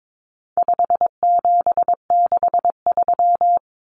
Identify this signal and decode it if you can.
Morse code, heard as a keyed tone, about 22 wpm, 700 Hz. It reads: 5763
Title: Common Cut Numbers - 4-Digit 22wpm